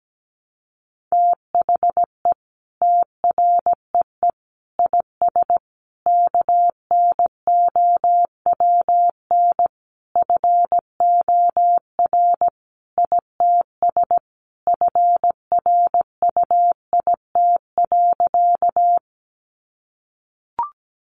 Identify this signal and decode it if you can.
Morse code, heard as a keyed tone, about 17 wpm, 700 Hz. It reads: THE TREE IS KNOWN FOR ITS FRUIT.
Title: Sentences from Top 1300 Words 17wpm